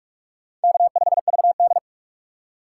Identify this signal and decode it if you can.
Morse code, heard as a keyed tone, about 46 wpm, 700 Hz. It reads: K5VD
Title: USA Call Signs - 4 Characters - 6x Character Spacing - 45wpm